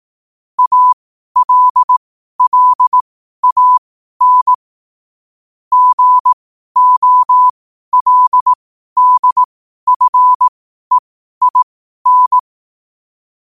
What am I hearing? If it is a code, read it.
Morse code, heard as a keyed tone, about 18 wpm, spaced as for 12 wpm, 1000 Hz. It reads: ALLAN GOLDFEIN
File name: qotdfiles / 12wpm